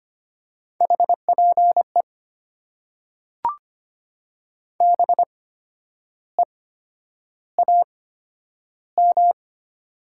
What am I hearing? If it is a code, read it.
Morse code, heard as a keyed tone, about 25 wpm, spaced as for 10 wpm, 700 Hz. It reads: HPE BEAM
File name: Top 90 QSO Elements - 8x Character Spacing - 25wpm